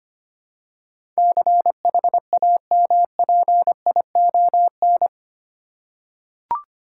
Morse code, heard as a keyed tone, about 25 wpm, 700 Hz. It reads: CHAMPION
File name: 8 Letter Words - First 3 Letters Given - 25wpm